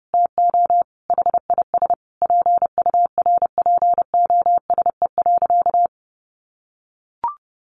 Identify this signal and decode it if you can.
Morse code, heard as a keyed tone, about 30 wpm, 700 Hz. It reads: TO HIS PURPOSE.
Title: Sentences from Top 1400 Words 30wpm